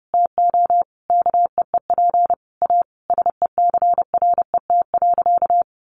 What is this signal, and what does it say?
Morse code, heard as a keyed tone, about 30 wpm, 700 Hz. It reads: TO KEEP A SECRET.
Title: Sentences from Top 1300 Words 30wpm